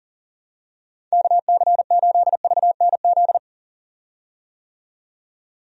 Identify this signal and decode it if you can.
Morse code, heard as a keyed tone, about 40 wpm, 700 Hz. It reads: KC8VNZ
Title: USA Call Signs - 6 Characters - 40wpm